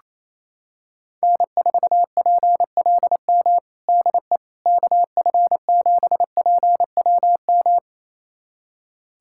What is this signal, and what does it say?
Morse code, heard as a keyed tone, about 28 wpm, 700 Hz. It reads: N4PLM DE KF7PWM
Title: USA Callsign DE Callsign 28wpm